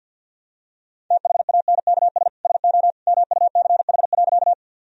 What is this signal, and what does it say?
Morse code, heard as a keyed tone, about 48 wpm, 700 Hz. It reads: THANKS SO MUCH.
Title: Sentences from Top 1000 Words - Review Part 3 - 50wpm